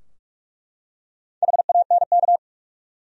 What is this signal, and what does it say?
Morse code, heard as a keyed tone, about 46 wpm, 700 Hz. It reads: HANK
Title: USA Common Names - Max 4 Letters - 45wpm